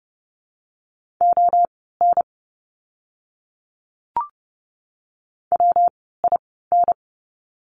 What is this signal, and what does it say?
Morse code, heard as a keyed tone, about 30 wpm, spaced as for 17 wpm, 700 Hz. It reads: ON WIN
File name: course / 30:10wpm / 115 Lesson 19f - Letter w - Words 30wpm